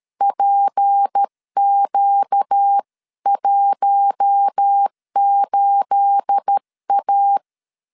Morse code, heard as a keyed tone, about 13 wpm, 800 Hz. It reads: PQ18A
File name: LEZIONI / 9 28 12wpm 2351